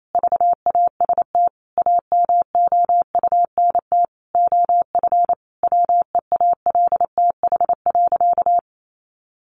Morse code, heard as a keyed tone, about 28 wpm, 700 Hz. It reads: VAST AMOUNT OF WEALTH.
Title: Sentences from Top 1300 Words 28wpm